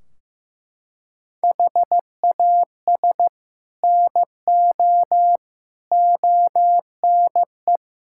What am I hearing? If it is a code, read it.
Morse code, heard as a keyed tone, about 15 wpm, 700 Hz. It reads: HAS NO ONE
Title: Sentences from Top 1600 Words 15wpm